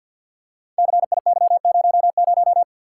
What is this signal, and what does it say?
Morse code, heard as a keyed tone, about 48 wpm, 700 Hz. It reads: CIY00
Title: Code Groups - 5 Letter:Numbers - 50wpm